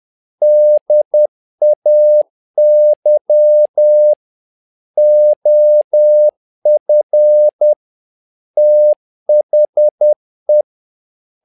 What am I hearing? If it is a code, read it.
Morse code, heard as a keyed tone, about 10 wpm, 600 Hz. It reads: DAY OF THE
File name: learning center / tower trouble / tower trouble 10wpm